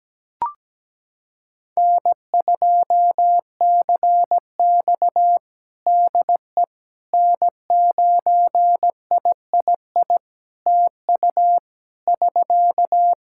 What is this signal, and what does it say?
Morse code, heard as a keyed tone, about 17 wpm, 700 Hz. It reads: N2CX DE N9III TU <SK>
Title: K1USN SST QSO Practice 17wpm